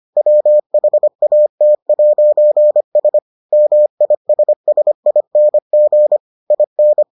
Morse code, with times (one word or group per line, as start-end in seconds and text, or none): 0.2-3.2 WHAT'S
3.5-6.2 MISSING
6.5-7.0 IN